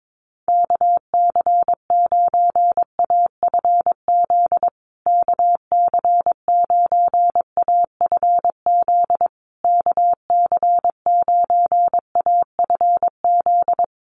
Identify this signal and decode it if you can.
Morse code, heard as a keyed tone, about 22 wpm, 700 Hz. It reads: KC9AFZ KC9AFZ KC9AFZ
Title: USA Call Signs Repeated 3x 22wpm